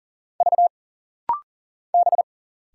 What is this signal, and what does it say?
Morse code, heard as a keyed tone, about 40 wpm, 700 Hz. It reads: V B
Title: Single Letters - Rapid-Fire 40wpm